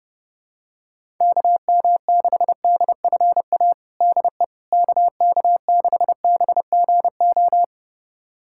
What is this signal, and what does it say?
Morse code, heard as a keyed tone, about 32 wpm, 700 Hz. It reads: KM6DFA DE KK6BGO
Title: USA Callsign DE Callsign - 6 Characters - 30wpm